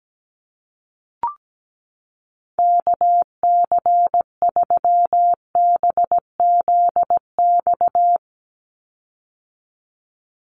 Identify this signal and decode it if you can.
Morse code, heard as a keyed tone, about 17 wpm, 700 Hz. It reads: KC3BZX
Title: USA Call Signs 17wpm